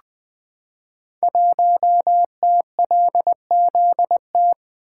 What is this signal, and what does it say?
Morse code, heard as a keyed tone, about 20 wpm, 700 Hz. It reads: 1TLZT
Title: Code Groups - 5 Letter:Numbers - 20wpm